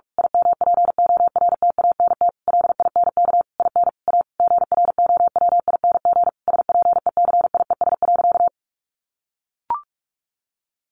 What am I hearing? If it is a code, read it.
Morse code, heard as a keyed tone, about 44 wpm, 700 Hz. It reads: IMPORTANT LINK IN A GROWING SPECIES.
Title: A Sound Of Thunder 45wpm